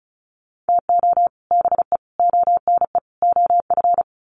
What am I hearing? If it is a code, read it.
Morse code, heard as a keyed tone, about 34 wpm, 700 Hz. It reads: TO BE ONE OF
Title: Sets of 4 Words 35wpm